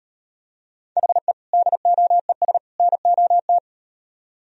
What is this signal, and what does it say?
Morse code, heard as a keyed tone, about 38 wpm, 700 Hz. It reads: HE DOES NOT